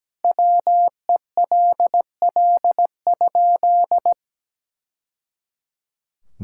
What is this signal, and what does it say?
Morse code, heard as a keyed tone, about 17 wpm, 700 Hz. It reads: WELL?
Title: Sentences from Top 1000 Words - Review Part 6 - 17wpm